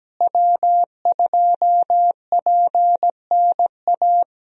Morse code, heard as a keyed tone, about 17 wpm, 700 Hz. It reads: W2PNA
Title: USA Call Signs - 5 Characters - Repeated 2x - 17wpm